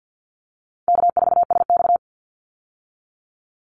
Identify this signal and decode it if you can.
Morse code, heard as a keyed tone, about 50 wpm, 700 Hz. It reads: K4SX
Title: Easy Call Signs Repeated 3x - 50wpm